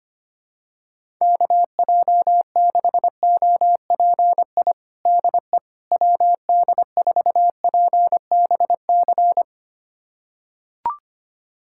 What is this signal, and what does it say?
Morse code, heard as a keyed tone, about 25 wpm, 700 Hz. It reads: KJ6OPI DE WD4PBC
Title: USA Callsign DE Callsign - 6 Characters - 25wpm